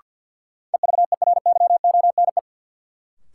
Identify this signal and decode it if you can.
Morse code, heard as a keyed tone, about 52 wpm, 700 Hz. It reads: EVERYONE
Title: Sentences from Top 800 Words - New Word Drill - 50wpm